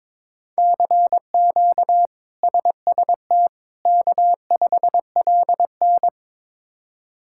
Morse code, heard as a keyed tone, about 21 wpm, 700 Hz. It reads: CQ SST K5LN
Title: K1USN SST QSO Practice - 22wpm